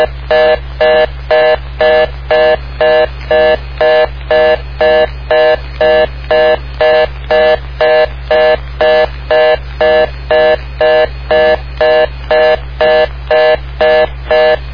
Reorder tone (North America), 480 + 620 Hz, with noise.